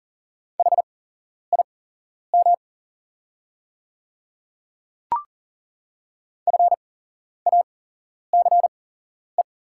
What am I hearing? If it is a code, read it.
Morse code, heard as a keyed tone, about 40 wpm, spaced as for 11 wpm, 690 Hz. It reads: HIM FACE
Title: 8x Character Spacing - Top 300 Words - 40wpm